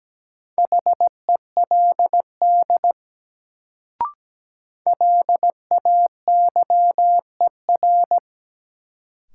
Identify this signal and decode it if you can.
Morse code, heard as a keyed tone, about 17 wpm, 700 Hz. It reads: HELD LAYER
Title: Sentences from Top 1600 Words - New Word Drill Alt - 17wpm